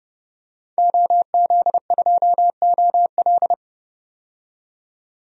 Morse code, heard as a keyed tone, about 30 wpm, 700 Hz. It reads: OZ2OL
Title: DXCC-100 Call Signs - 5 Characters - 30wpm